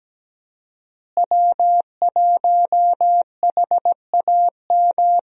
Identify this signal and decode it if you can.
Morse code, heard as a keyed tone, about 17 wpm, 700 Hz. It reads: W1HAM
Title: USA Call Signs - 5 Characters - Repeated 2x - 17wpm